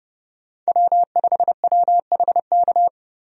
Morse code, heard as a keyed tone, about 30 wpm, 700 Hz. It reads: W5WHK
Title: USA Call Signs - 5 Characters - Repeated 3x - 30wpm